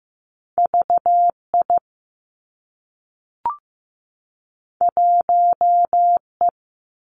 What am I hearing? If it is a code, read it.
Morse code, heard as a keyed tone, about 15 wpm, 700 Hz. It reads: V I 1 E
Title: Two Challenging Letter-Number Combinations 15wpm